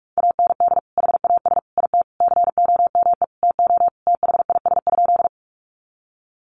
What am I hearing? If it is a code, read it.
Morse code, heard as a keyed tone, about 44 wpm, 700 Hz. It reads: AND HAS IT COME TO THIS?
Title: Sentences from Top 100 Words 45wpm